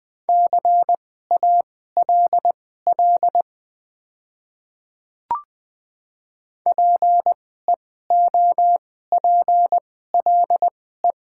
Morse code, heard as a keyed tone, about 20 wpm, spaced as for 14 wpm, 700 Hz. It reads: CALL PEOPLE
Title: 2x Character Spacing - Top 100 Words - 20wpm